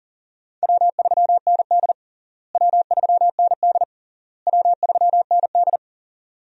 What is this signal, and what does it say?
Morse code, heard as a keyed tone, about 40 wpm, 700 Hz. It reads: W3ND W3ND W3ND
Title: USA Call Signs - 4 Characters - Repeated 3x - 40wpm